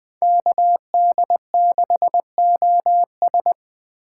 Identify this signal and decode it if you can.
Morse code, heard as a keyed tone, about 20 wpm, 700 Hz. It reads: KD6OS
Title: Speed Racing - USA 2x2 Call Signs 17wpm